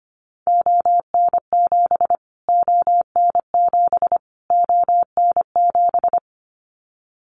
Morse code, heard as a keyed tone, about 25 wpm, 700 Hz. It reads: ON7 ON7 ON7
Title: Callsign Prefixes for Top 25 DXCC - Repeated 3x - 25wpm